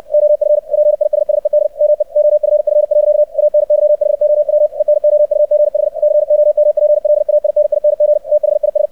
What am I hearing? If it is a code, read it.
Morse code, heard as a keyed tone, about 64 wpm, 600 Hz. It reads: OF PISSED ME OFF. ANYHOW ENOUGH COMPLAINING THIS
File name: archived / CRSnet